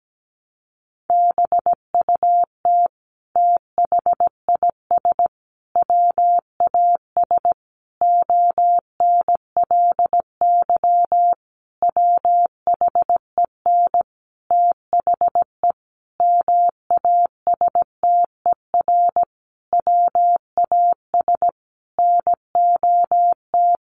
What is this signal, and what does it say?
Morse code, heard as a keyed tone, about 17 wpm, 700 Hz. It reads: BUT THIS WAS ONLY WHEN THE MASTER WAS NOT
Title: Sentences from Top 1300 Words 17wpm